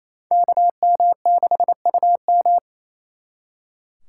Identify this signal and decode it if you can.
Morse code, heard as a keyed tone, about 37 wpm, 700 Hz. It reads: KM6UM